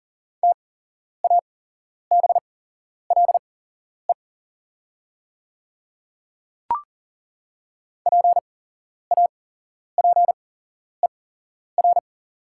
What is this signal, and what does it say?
Morse code, heard as a keyed tone, about 40 wpm, spaced as for 11 wpm, 690 Hz. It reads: TABLE PAPER
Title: 8x Character Spacing - Top 500 Words - 40wpm